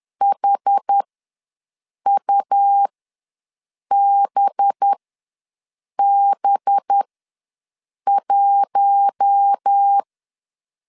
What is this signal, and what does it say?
Morse code, heard as a keyed tone, about 10 wpm, spaced as for 6 wpm, 800 Hz. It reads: HUBB1
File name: LEZIONI / 5 LBVC / 5 LB ws 10wpm 3616